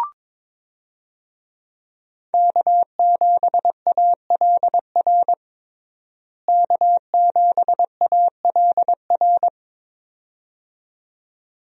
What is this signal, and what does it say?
Morse code, heard as a keyed tone, about 21 wpm, 700 Hz. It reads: K7ALR K7ALR
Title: USA Call Signs - 5 Characters - Repeated 2x - 22wpm